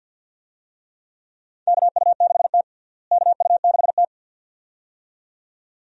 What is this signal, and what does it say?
Morse code, heard as a keyed tone, about 51 wpm, 700 Hz. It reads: KU6T KU6T
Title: USA Call Signs - 4 Characters - Repeated 2x - 50wpm